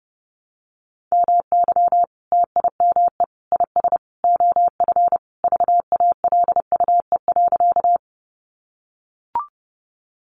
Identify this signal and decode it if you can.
Morse code, heard as a keyed tone, about 30 wpm, 700 Hz. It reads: MY TIME IS OF VALUE.
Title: Sentences from Top 600 Words 30wpm